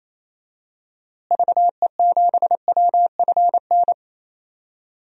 Morse code, heard as a keyed tone, about 27 wpm, 700 Hz. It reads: VE7WFN
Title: DXCC-100 Call Signs - 6 Characters - Repeated 2x - 28wpm